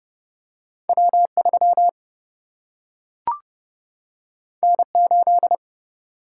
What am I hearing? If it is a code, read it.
Morse code, heard as a keyed tone, about 30 wpm, 700 Hz. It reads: W 3 N 8
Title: Single-Letter Single-Number 30wpm